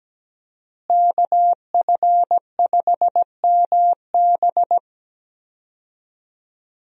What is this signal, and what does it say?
Morse code, heard as a keyed tone, about 17 wpm, 700 Hz. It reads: KF5MB
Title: USA Call Signs - 5 Characters - Repeated 2x - 17wpm